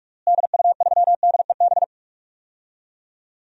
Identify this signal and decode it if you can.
Morse code, heard as a keyed tone, about 42 wpm, 700 Hz. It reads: DU3DEB
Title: DXCC-100 Call Signs - 6 Characters - Repeated 3x - 45wpm